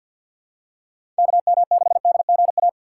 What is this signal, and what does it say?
Morse code, heard as a keyed tone, about 47 wpm, 700 Hz. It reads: KM6DGA
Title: USA Call Signs - 6 Characters - 6x Character Spacing - 50wpm